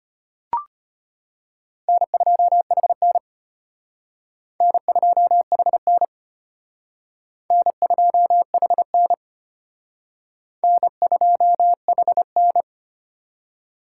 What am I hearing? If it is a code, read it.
Morse code, heard as a keyed tone, about 34 wpm, 700 Hz. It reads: N2HN N2HN N2HN N2HN